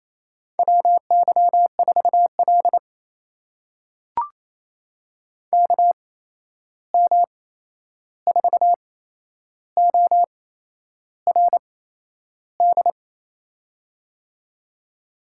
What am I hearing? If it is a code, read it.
Morse code, heard as a keyed tone, about 28 wpm, spaced as for 11 wpm, 700 Hz. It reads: WY4L KM4ORD
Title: USA Call Signs - 8x Character Spacing - 28wpm